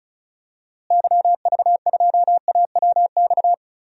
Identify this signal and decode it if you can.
Morse code, heard as a keyed tone, about 36 wpm, 700 Hz. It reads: YV2AWX
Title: DXCC-100 Call Signs - 6 Characters - Repeated 2x - 35wpm